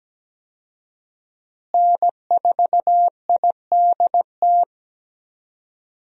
Morse code, heard as a keyed tone, about 17 wpm, 700 Hz. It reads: N4IDT
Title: USA Call Signs - 5 Characters - Repeated 3x - 17wpm